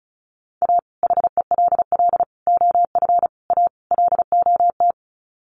Morse code, heard as a keyed tone, about 34 wpm, 700 Hz. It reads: A HELL OF A LOT
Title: Sets of 5 Words Encore 35wpm